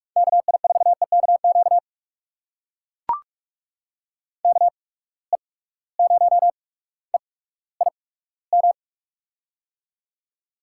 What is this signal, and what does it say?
Morse code, heard as a keyed tone, about 42 wpm, spaced as for 19 wpm, 700 Hz. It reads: KI4EKQ KE0EIM
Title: USA Call Signs - 6 Characters - 8x Character Spacing - 45wpm